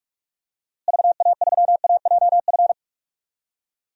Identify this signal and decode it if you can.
Morse code, heard as a keyed tone, about 46 wpm, 700 Hz. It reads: VA3AJF
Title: DXCC-100 Call Signs - 6 Characters - Repeated 3x - 45wpm